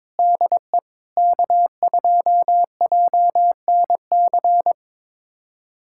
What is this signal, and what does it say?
Morse code, heard as a keyed tone, about 21 wpm, 700 Hz. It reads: DE K2JNC
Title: USA Callsign DE Callsign - 5 Characters - 22wpm